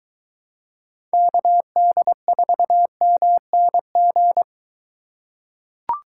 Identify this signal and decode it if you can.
Morse code, heard as a keyed tone, about 23 wpm, 700 Hz. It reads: KD4MNG